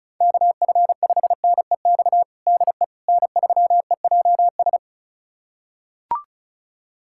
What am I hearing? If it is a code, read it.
Morse code, heard as a keyed tone, about 36 wpm, 700 Hz. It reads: KF5NEX DE N3EJS
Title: USA Callsign DE Callsign 35wpm